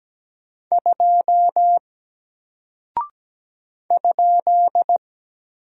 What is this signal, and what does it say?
Morse code, heard as a keyed tone, about 17 wpm, 700 Hz. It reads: 2 ?
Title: Single Character (Letter:Number:Punctuation:Prosign) Alt 17wpm